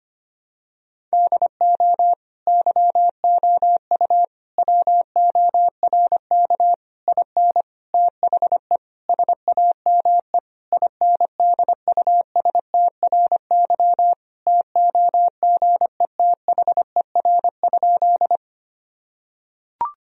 Morse code, heard as a keyed tone, about 25 wpm, 700 Hz. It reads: DO YOU WORK IN THE SAME INDUSTRY TOGETHER?
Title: Sentences from Top 1000 Words - Review Part 4 - 25wpm